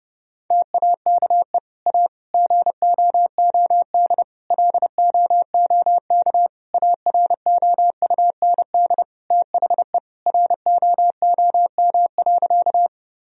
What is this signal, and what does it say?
Morse code, heard as a keyed tone, about 30 wpm, 700 Hz. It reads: TAKE A GOOD LOOK AROUND THE ROOM.
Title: Sentences from Top 1000 Words - Review Part 3 - 30wpm